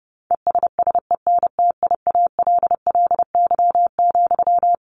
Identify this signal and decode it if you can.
Morse code, heard as a keyed tone, about 28 wpm, 700 Hz. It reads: ESSENTIALLY,
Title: Sentences from Top 1300 Words 30wpm